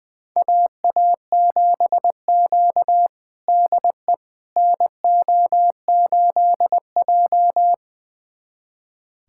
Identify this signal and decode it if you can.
Morse code, heard as a keyed tone, about 20 wpm, 700 Hz. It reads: AA7Q DE NO8J
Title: USA Callsign DE Callsign - 4 Characters - 20wpm